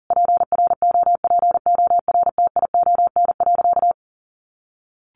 Morse code, heard as a keyed tone, about 40 wpm, 700 Hz. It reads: PROPORTION.
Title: A Sound Of Thunder 40wpm